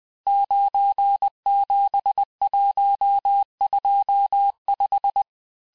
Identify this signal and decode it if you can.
Morse code, heard as a keyed tone, about 20 wpm, 800 Hz. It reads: 97125
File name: cours-858-texte-chiffre-vitesse-19mpm